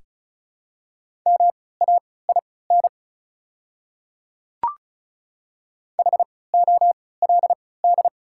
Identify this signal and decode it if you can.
Morse code, heard as a keyed tone, about 36 wpm, spaced as for 20 wpm, 690 Hz. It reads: MAIN HOLD
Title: 3x Character Spacing - Top 500 Words - 35wpm